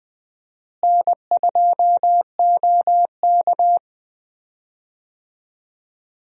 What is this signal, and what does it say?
Morse code, heard as a keyed tone, about 21 wpm, 700 Hz. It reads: N2OK
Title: USA Call Signs - 4 Characters - Repeated 3x - 20wpm